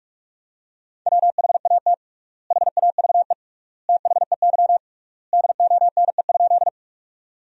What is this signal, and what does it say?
Morse code, heard as a keyed tone, about 46 wpm, 700 Hz. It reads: WHAT HAVE THEY DONE?
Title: 3x Word-Spacing - Sentences from Top 500 Words - 45wpm